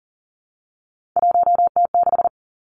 Morse code, heard as a keyed tone, about 42 wpm, 700 Hz. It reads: 1T6
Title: Three Challenging Letter-Number Combinations 40wpm